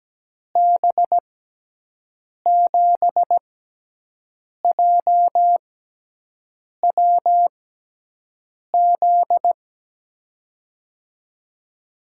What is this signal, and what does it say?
Morse code, heard as a keyed tone, about 17 wpm, spaced as for 6 wpm, 700 Hz. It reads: B7JWZ
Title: USA Call Signs - 6 Characters - 6x Character Spacing - 17wpm